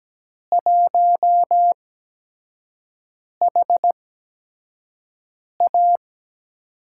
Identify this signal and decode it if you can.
Morse code, heard as a keyed tone, about 17 wpm, spaced as for 5 wpm, 700 Hz. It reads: 1HA
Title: USA Call Signs - 4 Characters - 8x Character Spacing - 17wpm